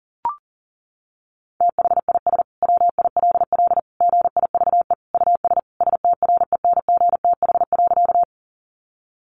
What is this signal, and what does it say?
Morse code, heard as a keyed tone, about 37 wpm, 700 Hz. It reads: THIS WILL GIVE US STRENGTH.
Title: Sentences from Top 1300 Words 40wpm